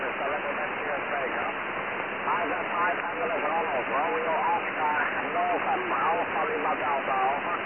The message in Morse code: none